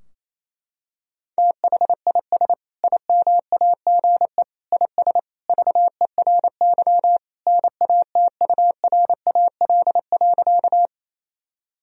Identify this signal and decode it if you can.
Morse code, heard as a keyed tone, about 28 wpm, 700 Hz. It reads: THIS IMAGE IS VERY NATURAL.
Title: Sentences from Top 1000 Words - Review Part 4 - 28wpm